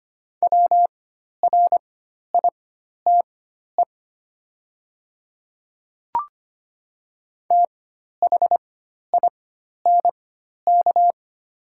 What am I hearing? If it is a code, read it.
Morse code, heard as a keyed tone, about 25 wpm, spaced as for 12 wpm, 700 Hz. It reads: WRITE THINK
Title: 4x Character Spacing - Top 500 Words - 25wpm